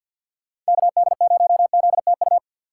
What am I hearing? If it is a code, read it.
Morse code, heard as a keyed tone, about 47 wpm, 700 Hz. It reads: KD0ZTU
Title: USA Call Signs - 6 Characters - 6x Character Spacing - 50wpm